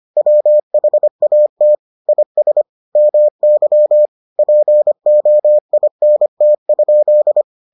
Morse code, heard as a keyed tone, about 25 wpm, 600 Hz. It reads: WHAT IS MY POINT?